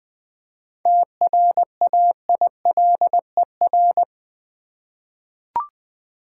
Morse code, heard as a keyed tone, about 20 wpm, 700 Hz. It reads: TRAILER
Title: 7 Letter Words - First 3 Letters Given - 20wpm